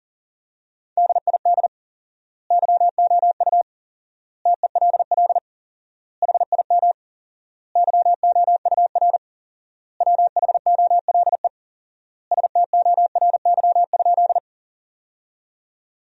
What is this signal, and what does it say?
Morse code, heard as a keyed tone, about 40 wpm, 700 Hz. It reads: DID YOU TELL HIM YOUR WHOLE STORY?
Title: 4x Word-Spacing - Sentences from Top 400 Words - 40wpm